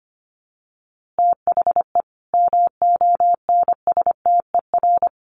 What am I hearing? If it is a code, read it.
Morse code, heard as a keyed tone, about 25 wpm, 700 Hz. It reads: THE MONSTER
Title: A Sound Of Thunder 25wpm